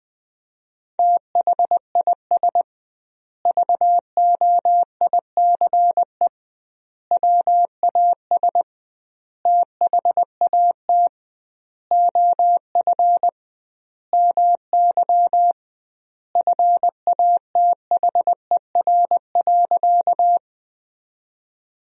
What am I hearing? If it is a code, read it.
Morse code, heard as a keyed tone, about 20 wpm, 700 Hz. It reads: THIS VOICE WAS THAT OF MY FATHER.
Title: 2x Word-Spacing - Sentences from Top 500 Words - 20wpm